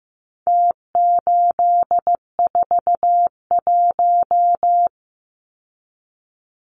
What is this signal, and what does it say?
Morse code, heard as a keyed tone, about 15 wpm, 700 Hz. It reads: T841
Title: Numbers - Contest Style - 4-Digit 15wpm